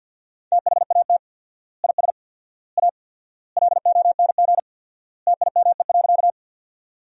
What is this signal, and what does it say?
Morse code, heard as a keyed tone, about 50 wpm, 700 Hz. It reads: THAT IS A LONG TIME.
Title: 4x Word-Spacing - Sentences from Top 100 Words - 50wpm